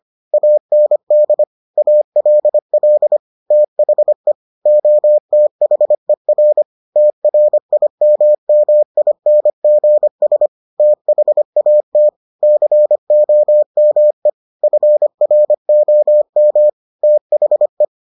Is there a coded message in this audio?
Yes: Morse code from 0.3 to 17.9 s, heard as a keyed tone, about 25 wpm, 600 Hz. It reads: AND ALL THE OTHER TRIMMINGS THAT COME FROM THE